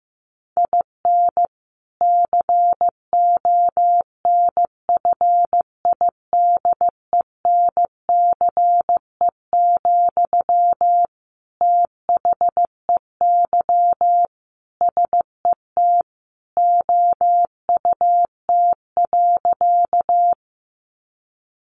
Morse code, heard as a keyed tone, about 15 wpm, 700 Hz. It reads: IN CONFIDENCE, THEY SET OUT.
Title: Sentences from Top 1500 Words 15wpm